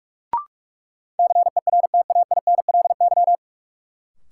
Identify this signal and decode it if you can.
Morse code, heard as a keyed tone, about 41 wpm, 700 Hz. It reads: CERTAINLY
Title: Sentences from Top 900 Words - New Word Drill - 45wpm